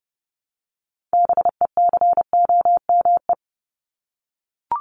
Morse code, heard as a keyed tone, about 29 wpm, 700 Hz. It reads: BECOME
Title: Top 501-750 Words 30wpm